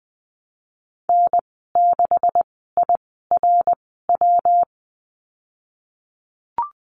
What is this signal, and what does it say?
Morse code, heard as a keyed tone, about 20 wpm, spaced as for 15 wpm, 700 Hz. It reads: N6IRW
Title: 154 Lesson 25g - Letter p - Call Signs 20wpm